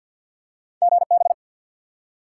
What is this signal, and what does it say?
Morse code, heard as a keyed tone, about 54 wpm, 700 Hz. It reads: GB
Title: Top 90 QSO Elements 50wpm